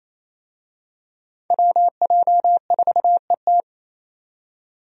Morse code, heard as a keyed tone, about 28 wpm, 700 Hz. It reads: WJ4ET
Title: USA Call Signs - 5 Characters - 28wpm